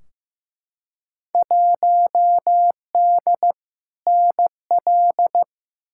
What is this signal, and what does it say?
Morse code, heard as a keyed tone, about 15 wpm, 700 Hz. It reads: 1D NL
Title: Field Day Exchanges Encore - 15wpm